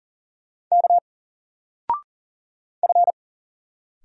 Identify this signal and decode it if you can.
Morse code, heard as a keyed tone, about 41 wpm, 700 Hz. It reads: K F
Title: Single Letter-Number Alt 40wpm